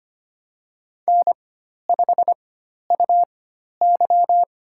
Morse code, heard as a keyed tone, about 25 wpm, spaced as for 12 wpm, 690 Hz. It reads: N5UY
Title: USA Call Signs - 4 Characters - 4x Character Spacing - 25wpm